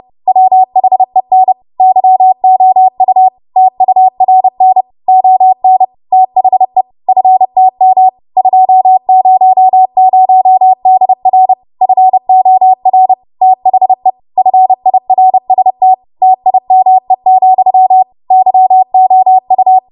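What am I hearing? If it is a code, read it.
Morse code, heard as a keyed tone, about 30 wpm, 750 Hz. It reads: WHEN YOU TURN ON THE FTM 200DR FOR THE FIRST TIME, YOU
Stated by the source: Morse practice transmission at 30 WPM.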